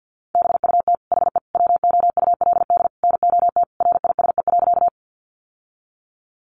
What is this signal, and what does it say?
Morse code, heard as a keyed tone, about 49 wpm, 700 Hz. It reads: BUT HE WOULD NOT RISE.
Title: Sentences from Top 900 Words 50wpm